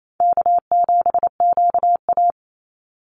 Morse code, heard as a keyed tone, about 28 wpm, 700 Hz. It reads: K7QA
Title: Easy Call Signs Repeated 2x - 28wpm